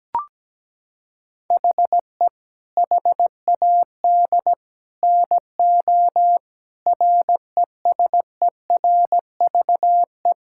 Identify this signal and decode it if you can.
Morse code, heard as a keyed tone, about 17 wpm, 700 Hz. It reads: HE HAD NO RESERVE
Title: Sentences from Top 1000 Words - Review Part 6 - 17wpm